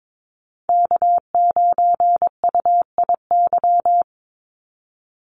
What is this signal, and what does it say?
Morse code, heard as a keyed tone, about 22 wpm, 700 Hz. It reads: K9UIY
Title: Call Signs Repeated 3x - 22wpm